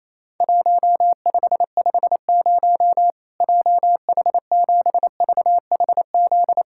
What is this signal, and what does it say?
Morse code, heard as a keyed tone, about 28 wpm, 700 Hz. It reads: 1550 JH7VHZ
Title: Worked All Europe DX Contest - QTC Exchange - 28wpm